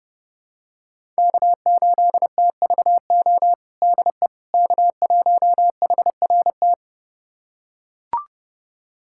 Morse code, heard as a keyed tone, about 32 wpm, 700 Hz. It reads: K8TVO DE K1HRT
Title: USA Callsign DE Callsign - 5 Characters - 30wpm